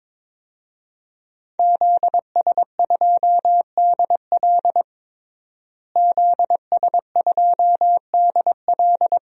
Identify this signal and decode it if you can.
Morse code, heard as a keyed tone, about 22 wpm, 700 Hz. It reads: ZS2DL ZS2DL
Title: DXCC-100 Call Signs - 5 Characters - Repeated 2x - 22wpm